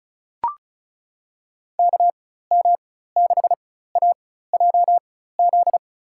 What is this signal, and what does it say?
Morse code, heard as a keyed tone, about 34 wpm, spaced as for 16 wpm, 700 Hz. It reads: KM6AJZ
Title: USA Call Signs - 6 Characters - 4x Character Spacing - 35wpm